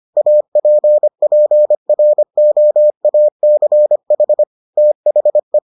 Morse code, heard as a keyed tone, about 25 wpm, 600 Hz. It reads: APPROACH THE